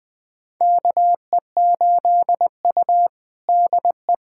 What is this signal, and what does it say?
Morse code, heard as a keyed tone, about 20 wpm, 700 Hz. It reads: KE8U DE
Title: USA Callsign DE Callsign - 4 Characters - 20wpm